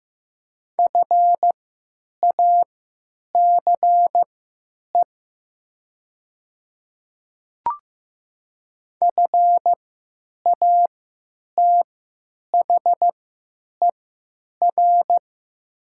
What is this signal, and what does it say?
Morse code, heard as a keyed tone, about 15 wpm, spaced as for 8 wpm, 700 Hz. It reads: FACE FATHER
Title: 3x Character Spacing - Top 300 Words - 15wpm